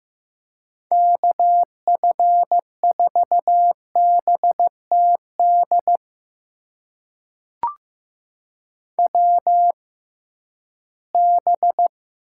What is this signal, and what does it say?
Morse code, heard as a keyed tone, about 15 wpm, spaced as for 11 wpm, 700 Hz. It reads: KF4BTD WB
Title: USA Call Signs - 6 Characters - 6x Character Spacing - 15wpm